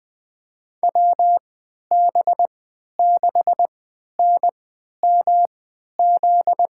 Morse code, heard as a keyed tone, about 20 wpm, spaced as for 11 wpm, 700 Hz. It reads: WB6NMZ
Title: USA Call Signs - 6 Characters - 3x Character Spacing - 20wpm